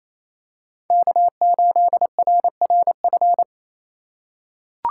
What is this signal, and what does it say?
Morse code, heard as a keyed tone, about 28 wpm, 700 Hz. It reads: K8RRF
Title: USA Call Signs - 8x Character Spacing - 28wpm